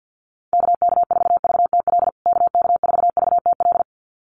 Morse code, heard as a keyed tone, about 48 wpm, 700 Hz. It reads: KK4VTL KK4VTL
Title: USA Call Signs Repeated 2x 50wpm